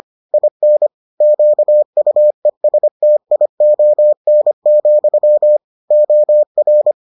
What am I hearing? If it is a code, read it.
Morse code, heard as a keyed tone, about 25 wpm, 600 Hz. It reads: IN QUESTION, OR